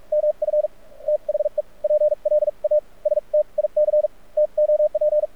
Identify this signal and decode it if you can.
Morse code, heard as a keyed tone, about 46 wpm, spaced as for 35 wpm, 600 Hz. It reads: MF THE PLASTIC TOP
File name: archived / CRSnet